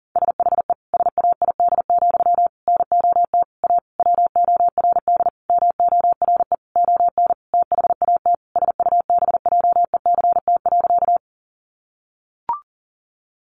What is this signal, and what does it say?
Morse code, heard as a keyed tone, about 37 wpm, 700 Hz. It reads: SHE SAID, NOT A WORD MORE ON THAT SUBJECT.
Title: Sentences from Top 800 Words 40wpm